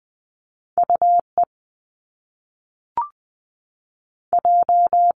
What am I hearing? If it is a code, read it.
Morse code, heard as a keyed tone, about 20 wpm, 700 Hz. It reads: U E J
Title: Top Two Letter Combinations in Words 20wpm